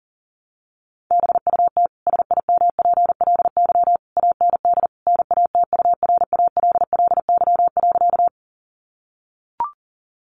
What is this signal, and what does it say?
Morse code, heard as a keyed tone, about 40 wpm, 700 Hz. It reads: BUT SIMPLY AND NATURALLY.
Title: Sentences from Top 1300 Words 40wpm